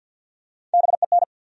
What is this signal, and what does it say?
Morse code, heard as a keyed tone, about 53 wpm, 700 Hz. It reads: BEN
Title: USA Common Names - Max 3 Letters - 50wpm